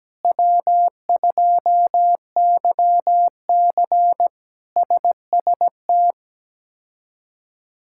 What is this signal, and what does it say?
Morse code, heard as a keyed tone, about 17 wpm, 700 Hz. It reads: W2YC SST
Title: K1USN SST QSO Practice - 17wpm